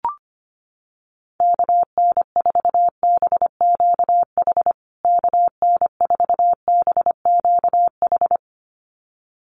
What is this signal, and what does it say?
Morse code, heard as a keyed tone, about 25 wpm, 700 Hz. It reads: KN4BQH KN4BQH
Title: USA Call Signs Repeated 2x 25wpm